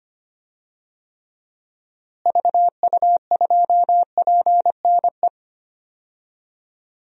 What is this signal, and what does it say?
Morse code, heard as a keyed tone, about 25 wpm, 700 Hz. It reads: VU2PNE
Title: DXCC-100 Call Signs - 6 Characters - Repeated 2x - 25wpm